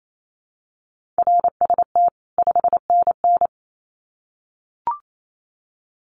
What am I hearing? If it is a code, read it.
Morse code, heard as a keyed tone, about 28 wpm, 700 Hz. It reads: RST 5NN
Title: Common QSO Phrases 28wpm